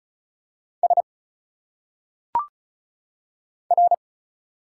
Morse code, heard as a keyed tone, about 35 wpm, 700 Hz. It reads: S R
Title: Single Letters - Phonetic Pronunciation 35wpm